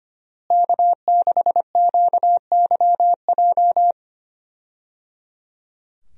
Morse code, heard as a keyed tone, about 25 wpm, 700 Hz. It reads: K6QYJ
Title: USA Call Signs - 5 Characters - Repeated 3x - 25wpm